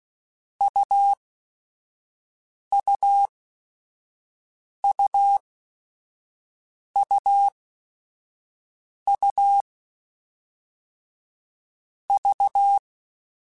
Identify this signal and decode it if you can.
Morse code, heard as a keyed tone, about 16 wpm, spaced as for 5 wpm, 800 Hz. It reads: UUUUU V
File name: Short Runs / Short Runs-mp3s / AUV AUV4 @5